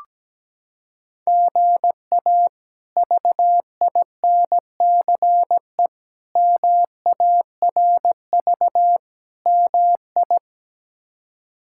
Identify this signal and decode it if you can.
Morse code, heard as a keyed tone, about 17 wpm, 700 Hz. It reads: GA VINCE MARV MI
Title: K1USN SST QSO Practice - 17wpm